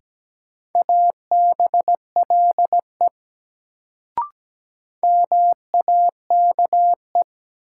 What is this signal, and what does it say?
Morse code, heard as a keyed tone, about 17 wpm, 700 Hz. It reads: ABLE MAKE
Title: Sets of 2 Words - New Word Drill - 17wpm